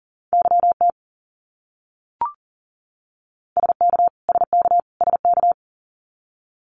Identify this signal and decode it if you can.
Morse code, heard as a keyed tone, about 40 wpm, 700 Hz. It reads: YT SK SK SK
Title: Canadian Provinces and Territories - Repeated 3x - 40wpm